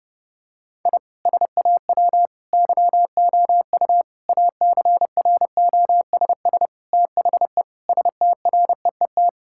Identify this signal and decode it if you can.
Morse code, heard as a keyed tone, about 30 wpm, 700 Hz. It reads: I SAW YOU ACROSS THE STREET
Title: Sentences from Top 1000 Words - Review Part 5 - 30wpm